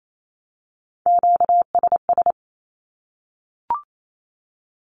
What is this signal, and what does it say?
Morse code, heard as a keyed tone, about 28 wpm, 700 Hz. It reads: QSS
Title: Q-Codes 28wpm